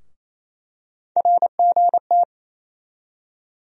Morse code, heard as a keyed tone, about 28 wpm, 700 Hz. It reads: RGT
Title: USA Call Sign Postfixes - 3 Letters 28wpm